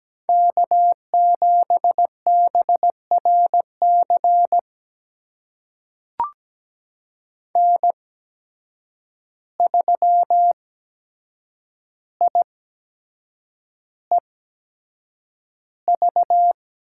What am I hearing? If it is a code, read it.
Morse code, heard as a keyed tone, about 17 wpm, spaced as for 7 wpm, 700 Hz. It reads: K7BRC N3IEV
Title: USA Call Signs - 8x Character Spacing - 17wpm